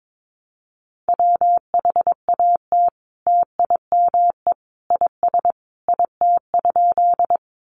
Morse code, heard as a keyed tone, about 22 wpm, 700 Hz. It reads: WHAT TIME IS IT?
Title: Sentences from Top 100 Words 22wpm